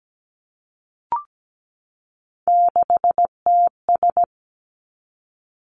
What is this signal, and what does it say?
Morse code, heard as a keyed tone, about 17 wpm, 700 Hz. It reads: 6TS
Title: Three Challenging Letter-Number Combinations 17wpm